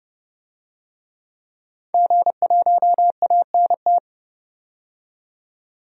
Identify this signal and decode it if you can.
Morse code, heard as a keyed tone, about 30 wpm, 700 Hz. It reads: G1ANT
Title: DXCC-100 Call Signs - 5 Characters - 30wpm